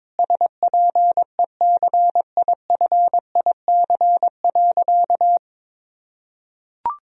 Morse code, heard as a keyed tone, about 21 wpm, 700 Hz. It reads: SPECIFIC.
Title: Sentences from Top 1000 Words - Review Part 3 - 22wpm